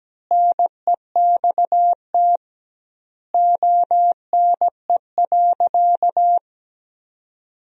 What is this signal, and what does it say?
Morse code, heard as a keyed tone, about 17 wpm, 700 Hz. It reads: NEXT ONE.
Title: 2x Word-Spacing - Sentences from Top 300 Words - 17wpm